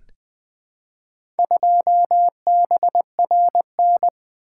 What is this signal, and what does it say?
Morse code, heard as a keyed tone, about 20 wpm, 700 Hz. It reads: 2BRN
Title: Four Letter-Number Combinations - 20wpm